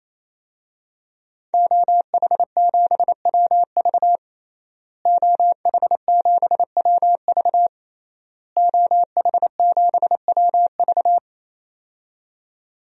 Morse code, heard as a keyed tone, about 28 wpm, 700 Hz. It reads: OH7WV OH7WV OH7WV
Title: DXCC-100 Call Signs - 5 Characters - Repeated 3x - 28wpm